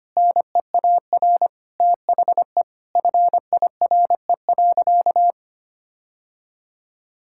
Morse code, heard as a keyed tone, about 25 wpm, 700 Hz. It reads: NEAR THE FIRE.
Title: Sentences from Top 1600 Words 25wpm